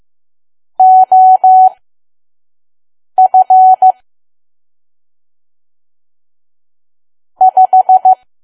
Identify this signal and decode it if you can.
Morse code, heard as a keyed tone, about 15 wpm, spaced as for 5 wpm, 750 Hz. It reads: OF 5
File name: Archive / 5 WPM files / PM